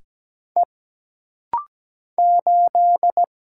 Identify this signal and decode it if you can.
Morse code, heard as a keyed tone, about 17 wpm, 700 Hz. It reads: E 8
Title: Single Letter-Number Alt - Rapid-Fire 17wpm